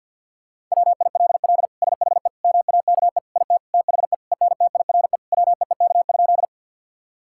Morse code, heard as a keyed tone, about 50 wpm, 700 Hz. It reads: WILL SHE MAKE IT THE ENTIRE WEEK?
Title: Sentences from Top 1000 Words - Review Part 5 - 50wpm